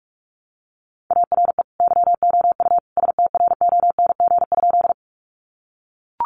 Morse code, heard as a keyed tone, about 44 wpm, 700 Hz. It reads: ARE YOU STRONG?
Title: Sentences from Top 500 Words 45wpm